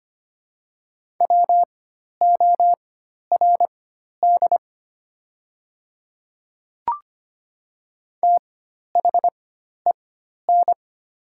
Morse code, heard as a keyed tone, about 25 wpm, spaced as for 12 wpm, 700 Hz. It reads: WORD THEN
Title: 4x Character Spacing - Top 100 Words - 25wpm